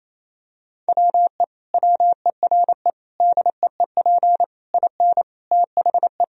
Morse code, heard as a keyed tone, about 27 wpm, 700 Hz. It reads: WE WERE DEEP IN THE
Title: Sentences from Top 1000 Words - Review Part 3 - 28wpm